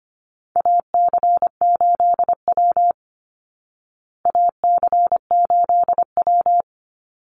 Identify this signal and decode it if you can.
Morse code, heard as a keyed tone, about 25 wpm, 700 Hz. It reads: AC8W AC8W
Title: Call Signs Repeated 2x - 25wpm